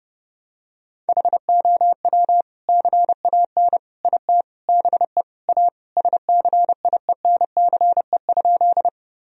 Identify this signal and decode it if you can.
Morse code, heard as a keyed tone, about 32 wpm, 700 Hz. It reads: HOW CAN IT BE A SCIENCE?
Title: Sentences from Top 1000 Words - Review Part 2 - 30wpm